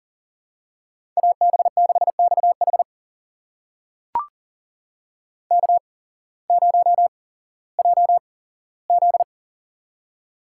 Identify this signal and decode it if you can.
Morse code, heard as a keyed tone, about 40 wpm, spaced as for 19 wpm, 700 Hz. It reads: AB6XH K0JZ
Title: USA Call Signs - 8x Character Spacing - 40wpm